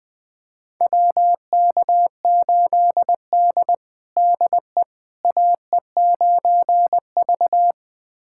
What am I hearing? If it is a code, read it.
Morse code, heard as a keyed tone, about 20 wpm, 700 Hz. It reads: WK8D DE AE9V
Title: USA Callsign DE Callsign - 4 Characters - 20wpm